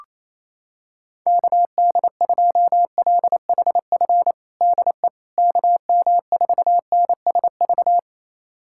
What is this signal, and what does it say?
Morse code, heard as a keyed tone, about 28 wpm, 700 Hz. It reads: KD2LHF DE KM4NSV
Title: USA Callsign DE Callsign - 6 Characters - 28wpm